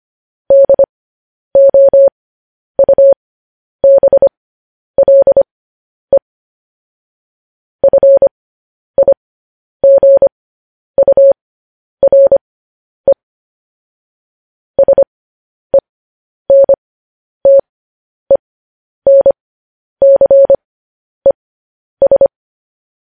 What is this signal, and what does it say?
Morse code, heard as a keyed tone, about 25 wpm, spaced as for 10 wpm, 550 Hz. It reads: DOUBLE FIGURE SENTENCES
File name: podcast / News headlines-1x-25x10